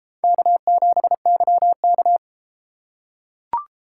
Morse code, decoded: K7YK

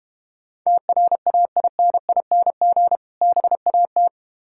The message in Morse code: TRAINING BAT

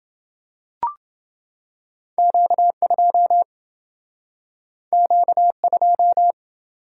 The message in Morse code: Q2 Q2

